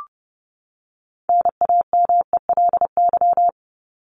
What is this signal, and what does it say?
Morse code, heard as a keyed tone, about 30 wpm, 700 Hz. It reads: NAMELY
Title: LY Suffix - Top 100 Words with Base Words 30wpm